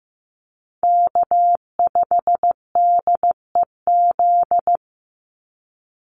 Morse code, heard as a keyed tone, about 15 wpm, 700 Hz. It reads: K5DEZ
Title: NA SOTA Chaser Callsigns 15wpm